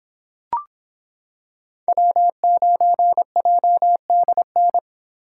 Morse code, heard as a keyed tone, about 25 wpm, 700 Hz. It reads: W9JDN